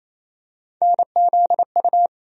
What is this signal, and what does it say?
Morse code, heard as a keyed tone, about 28 wpm, 700 Hz. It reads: NZU
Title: USA Call Sign Postfixes - 3 Letters 28wpm